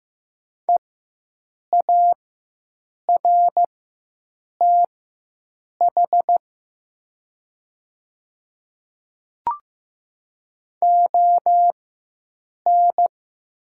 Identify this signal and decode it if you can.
Morse code, heard as a keyed tone, about 15 wpm, spaced as for 7 wpm, 700 Hz. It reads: EARTH ON
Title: 4x Character Spacing - Top 200 Words - 15wpm